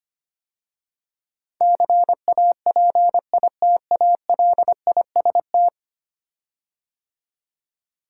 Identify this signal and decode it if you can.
Morse code, heard as a keyed tone, about 25 wpm, 700 Hz. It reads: CAPITALIST
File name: 10 Letter Words - First 3 Letters Given - 25wpm